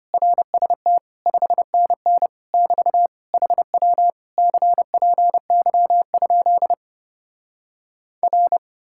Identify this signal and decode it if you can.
Morse code, heard as a keyed tone, about 33 wpm, 700 Hz. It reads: RST 5NN <BT> HW CPY? R